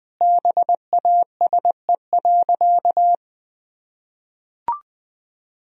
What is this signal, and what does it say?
Morse code, heard as a keyed tone, about 21 wpm, 700 Hz. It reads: BASE.
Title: Sentences from Top 1000 Words - Review Part 1 - 20wpm